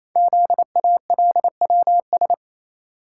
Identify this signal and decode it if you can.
Morse code, heard as a keyed tone, about 29 wpm, 700 Hz. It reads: ZALWS
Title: Code Groups - 5 Letters - 28wpm